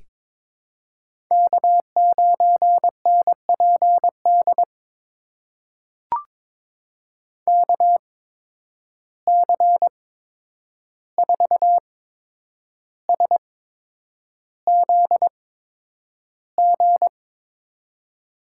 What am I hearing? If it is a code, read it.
Morse code, heard as a keyed tone, about 22 wpm, spaced as for 9 wpm, 700 Hz. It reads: K9NPD KC4SZG
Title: USA Call Signs - 8x Character Spacing - 22wpm